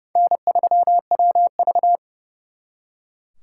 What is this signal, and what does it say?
Morse code, heard as a keyed tone, about 38 wpm, 700 Hz. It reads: N3WV